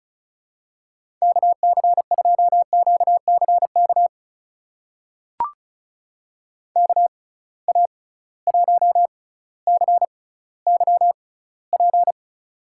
Morse code, heard as a keyed tone, about 36 wpm, spaced as for 18 wpm, 700 Hz. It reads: KC2QCK KA1CYP
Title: USA Call Signs - 6 Characters - 6x Character Spacing - 35wpm